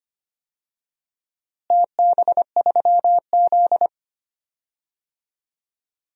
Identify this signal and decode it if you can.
Morse code, heard as a keyed tone, about 25 wpm, 700 Hz. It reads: TB3Z
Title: DXCC-100 Call Signs - 4 Characters - Repeated 3x - 25wpm